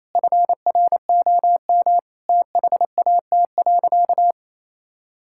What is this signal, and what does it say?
Morse code, heard as a keyed tone, about 28 wpm, 700 Hz. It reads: FROM THAT.
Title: Sentences from Top 1000 Words - Review Part 5 - 28wpm